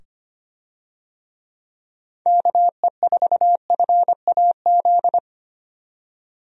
Morse code, heard as a keyed tone, about 25 wpm, 700 Hz. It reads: KE4FAZ
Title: USA Call Signs - 6 Characters - 25wpm